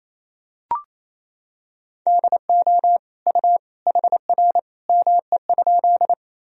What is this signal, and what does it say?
Morse code, heard as a keyed tone, about 28 wpm, 700 Hz. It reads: DO U HR ME?
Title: Sentences from Top 300 Words and Abbreviations 28wpm